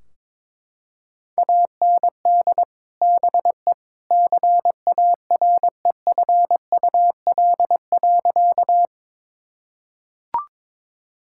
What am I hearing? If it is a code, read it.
Morse code, heard as a keyed tone, about 22 wpm, 700 Hz. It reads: AND BE CAREFUL.
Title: Sentences from Top 1600 Words 22wpm